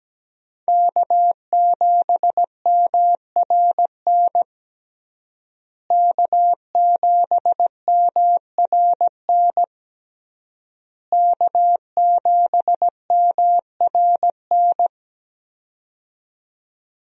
Morse code, heard as a keyed tone, about 17 wpm, 700 Hz. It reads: K7MRN K7MRN K7MRN
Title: USA Call Signs - 5 Characters - Repeated 3x - 17wpm